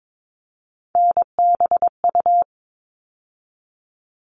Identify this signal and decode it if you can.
Morse code, heard as a keyed tone, about 22 wpm, 700 Hz. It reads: NBU
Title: All Cut Numbers - 3-Digit 22wpm